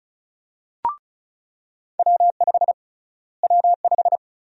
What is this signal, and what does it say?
Morse code, heard as a keyed tone, about 36 wpm, 700 Hz. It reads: W5 W5
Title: USA Call Sign Prefixes - Repeated 2x - 35wpm